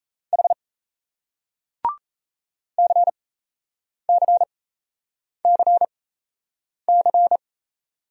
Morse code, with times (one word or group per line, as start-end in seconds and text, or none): 0.3-0.5 H
2.8-7.4 CCCC